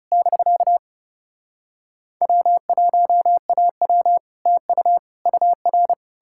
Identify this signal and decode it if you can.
Morse code, heard as a keyed tone, about 40 wpm, 700 Hz. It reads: <BK> W1AW TU UR